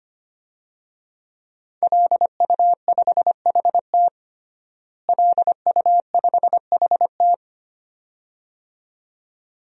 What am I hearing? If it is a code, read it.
Morse code, heard as a keyed tone, about 25 wpm, 700 Hz. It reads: LU5HT LU5HT
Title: DXCC-100 Call Signs - 5 Characters - Repeated 2x - 25wpm